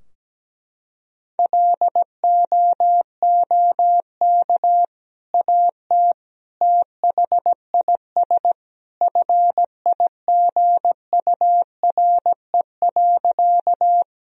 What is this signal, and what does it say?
Morse code, heard as a keyed tone, about 17 wpm, 700 Hz. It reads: LOOK AT THIS FIGURE.
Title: Sentences from Top 1000 Words - Review Part 2 - 17wpm